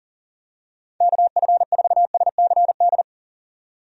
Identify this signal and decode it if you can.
Morse code, heard as a keyed tone, about 40 wpm, 700 Hz. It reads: KF4SCD
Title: USA Call Signs - 6 Characters - 40wpm